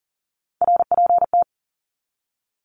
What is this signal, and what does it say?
Morse code, heard as a keyed tone, about 40 wpm, 700 Hz. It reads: RPT
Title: Abbreviations 40wpm